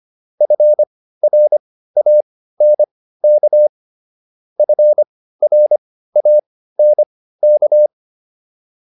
Morse code, heard as a keyed tone, about 25 wpm, spaced as for 15 wpm, 600 Hz. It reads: FRANK FRANK
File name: QSO208 15